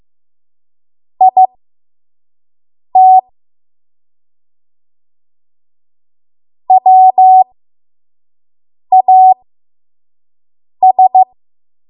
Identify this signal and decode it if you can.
Morse code, heard as a keyed tone, about 15 wpm, spaced as for 5 wpm, 750 Hz. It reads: IT WAS
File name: Morse / 05WPM